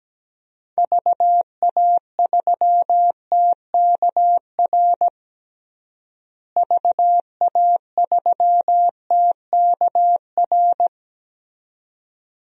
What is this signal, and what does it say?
Morse code, heard as a keyed tone, about 17 wpm, 700 Hz. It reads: VA3TKR VA3TKR
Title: DXCC-100 Call Signs - 6 Characters - Repeated 2x - 17wpm